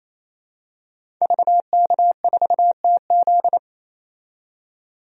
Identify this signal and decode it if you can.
Morse code, heard as a keyed tone, about 28 wpm, 700 Hz. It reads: VK4TZ
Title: DXCC-100 Call Signs - 5 Characters - 28wpm